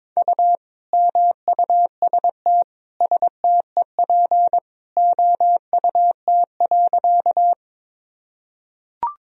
Morse code, heard as a keyed tone, about 22 wpm, 700 Hz. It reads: U MUST STEP OUT.
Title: Sentences from Top 400 Words and Abbreviations 22wpm